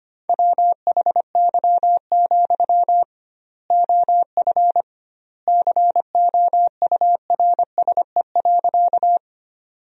Morse code, heard as a keyed tone, about 25 wpm, 700 Hz. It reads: WHY, OF COURSE.
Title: 2x Word-Spacing - Sentences from Top 500 Words - 25wpm